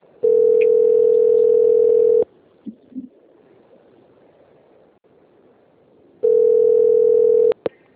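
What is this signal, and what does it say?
Ringback tone (North America), 440 + 480 Hz.
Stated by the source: Outgoing telephone call recorded on the Rio Claro system.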